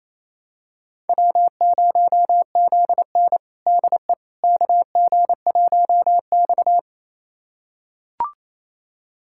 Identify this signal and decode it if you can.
Morse code, heard as a keyed tone, about 28 wpm, 700 Hz. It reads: W0ZN DE KG1X
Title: USA Callsign DE Callsign - 4 Characters - 28wpm